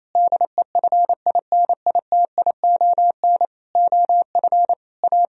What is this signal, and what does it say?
Morse code, heard as a keyed tone, about 27 wpm, 700 Hz. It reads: DEFINITION OF A
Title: Sentences from Top 1000 Words - Review Part 4 - 28wpm